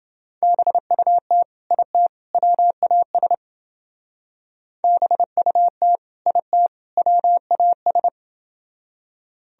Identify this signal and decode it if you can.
Morse code, heard as a keyed tone, about 30 wpm, 700 Hz. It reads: BUT IT WAS BUT IT WAS